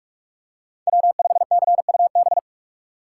Morse code, heard as a keyed tone, about 45 wpm, 700 Hz. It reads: W5CUB
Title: USA Call Signs - 5 Characters - 4x Character Spacing - 45wpm